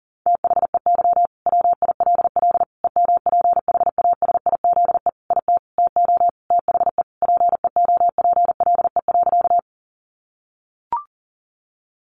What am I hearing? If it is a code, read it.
Morse code, heard as a keyed tone, about 40 wpm, 700 Hz. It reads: THEY WILL EMPHASIZE IT TO THE PEOPLE.
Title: Sentences from Top 1500 Words 40wpm